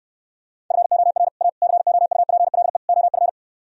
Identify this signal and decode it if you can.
Morse code, heard as a keyed tone, about 68 wpm, 700 Hz. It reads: FOR A COUPLE OF